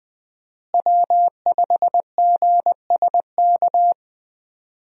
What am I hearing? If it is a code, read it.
Morse code, heard as a keyed tone, about 21 wpm, 700 Hz. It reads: W5GSK
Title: USA Call Signs - 5 Characters - Repeated 2x - 20wpm